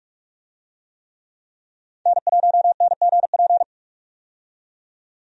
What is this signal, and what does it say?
Morse code, heard as a keyed tone, about 44 wpm, 700 Hz. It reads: N1NGP
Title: USA Call Signs - 5 Characters - 45wpm